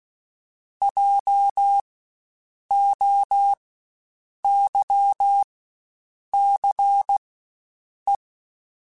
Morse code, heard as a keyed tone, about 16 wpm, spaced as for 7 wpm, 800 Hz. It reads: JOYCE
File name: Spaced Alphas / Spaced Alphas mp3s / Js sp @15